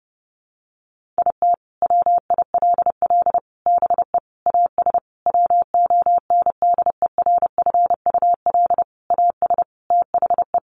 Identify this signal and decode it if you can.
Morse code, heard as a keyed tone, about 28 wpm, 700 Hz. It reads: IT WILL BE AS WONDERFUL AS THE
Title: Sentences from Top 1400 Words 30wpm